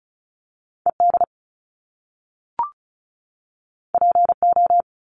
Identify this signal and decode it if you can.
Morse code, heard as a keyed tone, about 35 wpm, 700 Hz. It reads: ED PO
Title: Top Two Letter Combinations in Words 35wpm